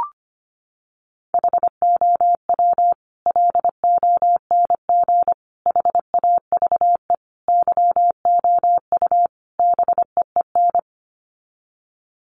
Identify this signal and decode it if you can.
Morse code, heard as a keyed tone, about 25 wpm, 700 Hz. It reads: HOW LONG HAVE YOU BEEN
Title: Sets of 5 Words Encore 25wpm